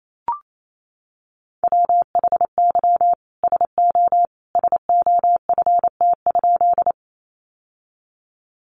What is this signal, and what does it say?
Morse code, heard as a keyed tone, about 28 wpm, 700 Hz. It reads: WHY SO SOFT?
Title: Sentences from Top 1500 Words 28wpm